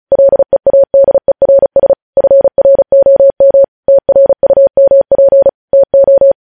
Morse code, heard as a keyed tone, about 36 wpm, 550 Hz. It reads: LEADERS FROM TRUMP TO
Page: news headlines in morse code at 35 wpm